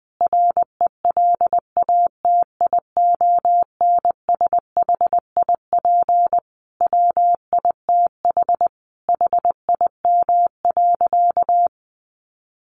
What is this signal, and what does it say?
Morse code, heard as a keyed tone, about 19 wpm, 700 Hz. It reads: RELATIONSHIP WITH HIM.
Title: Sentences from Top 1100 Words 20wpm